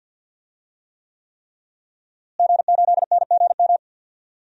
Morse code, heard as a keyed tone, about 51 wpm, 700 Hz. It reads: G8NGM
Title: DXCC-100 Call Signs - 5 Characters - Repeated 3x - 50wpm